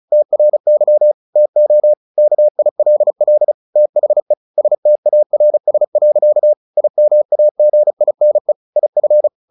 Morse code, heard as a keyed tone, about 34 wpm, 600 Hz. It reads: TRY TO KILL THE STARS. IMAGINE IF
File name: the old man and the sea 2 35wpm